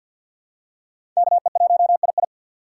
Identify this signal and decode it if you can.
Morse code, heard as a keyed tone, about 53 wpm, 700 Hz. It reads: KE1II
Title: USA Call Signs - 6x Character Spacing - 50wpm